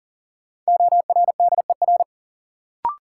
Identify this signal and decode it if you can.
Morse code, heard as a keyed tone, about 37 wpm, 700 Hz. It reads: ORDER
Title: Sentences from Top 400 Words - New Word Drill - 40wpm